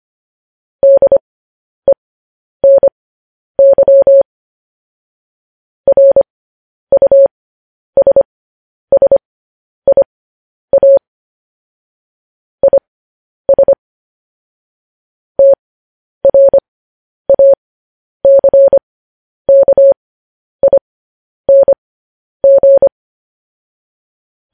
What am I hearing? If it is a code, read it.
Morse code, heard as a keyed tone, about 25 wpm, spaced as for 10 wpm, 550 Hz. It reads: DENY RUSSIA IS TRACKING
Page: news headlines in morse code at 10 wpm